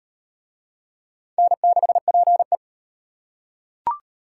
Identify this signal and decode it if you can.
Morse code, heard as a keyed tone, about 35 wpm, 700 Hz. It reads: N6PE